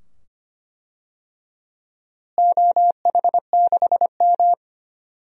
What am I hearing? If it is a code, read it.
Morse code, heard as a keyed tone, about 25 wpm, 700 Hz. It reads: OH6M
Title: DXCC-100 Call Signs - 4 Characters - Repeated 2x - 25wpm